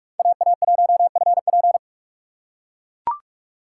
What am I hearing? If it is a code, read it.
Morse code, heard as a keyed tone, about 44 wpm, 690 Hz. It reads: AA1FP
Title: USA Call Signs - 4x Character Spacing - 45wpm